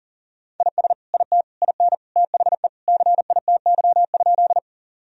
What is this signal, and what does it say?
Morse code, heard as a keyed tone, about 40 wpm, 700 Hz. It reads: IS IT IN THE CITY?
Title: Sentences from Top 1000 Words - Review Part 1 - 40wpm